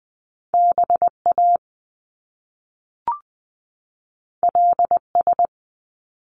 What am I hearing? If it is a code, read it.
Morse code, heard as a keyed tone, about 20 wpm, 700 Hz. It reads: B A L S
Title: Top Two Letter Combinations in Words 20wpm